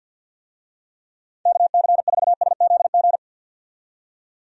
Morse code, heard as a keyed tone, about 51 wpm, 700 Hz. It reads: KC4SZG
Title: USA Call Signs - 6 Characters - 50wpm